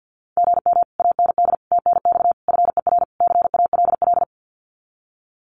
Morse code, heard as a keyed tone, about 52 wpm, 700 Hz. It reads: GM AND TNX FER CALL
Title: Common QSO Phrases Encore 50wpm